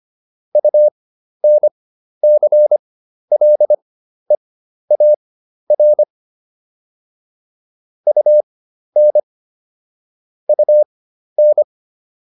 Morse code, heard as a keyed tone, about 25 wpm, spaced as for 12 wpm, 600 Hz. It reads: UNCLEAR UN UN